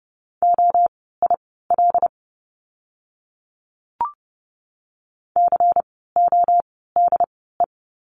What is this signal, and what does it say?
Morse code, heard as a keyed tone, about 30 wpm, spaced as for 17 wpm, 700 Hz. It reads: OIL CODE
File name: Lesson 16f - Letter c - Words 30wpm